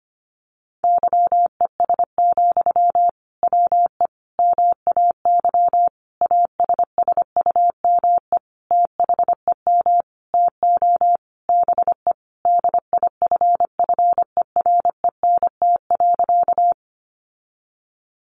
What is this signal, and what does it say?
Morse code, heard as a keyed tone, about 25 wpm, 700 Hz. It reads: YES, WE MAY ASSUME THEM TO BE DIFFERENT.
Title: Sentences from Top 1500 Words 25wpm